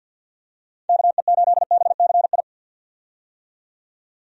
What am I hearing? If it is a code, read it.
Morse code, heard as a keyed tone, about 47 wpm, 700 Hz. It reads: KE8BCI
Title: USA Call Signs - 6 Characters - 50wpm